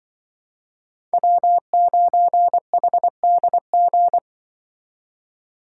Morse code, heard as a keyed tone, about 23 wpm, 700 Hz. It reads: W9HDG
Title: Speed Racing - USA 1x3 Call Signs 20wpm